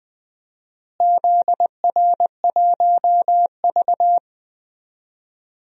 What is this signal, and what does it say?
Morse code, heard as a keyed tone, about 20 wpm, 700 Hz. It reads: ZR1V
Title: DXCC-100 Call Signs - 4 Characters - Repeated 3x - 20wpm